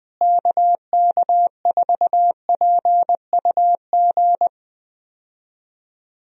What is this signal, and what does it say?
Morse code, heard as a keyed tone, about 20 wpm, 700 Hz. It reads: KK4PUG
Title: USA Call Signs - 6 Characters - Repeated 2x - 20wpm